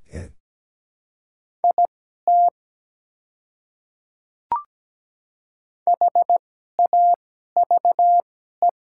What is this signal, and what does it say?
Morse code, heard as a keyed tone, about 17 wpm, spaced as for 12 wpm, 700 Hz. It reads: IT HAVE
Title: 2x Character Spacing - Top 100 Words - 17wpm